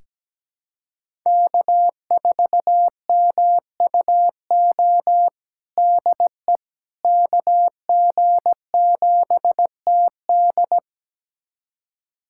Morse code, heard as a keyed tone, about 17 wpm, 700 Hz. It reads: K4MUO DE KG7TD
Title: USA Callsign DE Callsign - 5 Characters - 17wpm